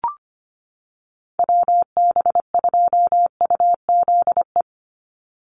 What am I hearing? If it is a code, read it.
Morse code, heard as a keyed tone, about 25 wpm, 700 Hz. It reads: WB2UZE
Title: Call Signs - 25wpm